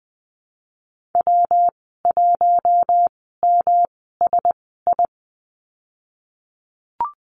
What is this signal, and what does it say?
Morse code, heard as a keyed tone, about 20 wpm, spaced as for 15 wpm, 700 Hz. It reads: W1MSI
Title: 122 Lesson 20g - Number 3 - Call Signs 20wpm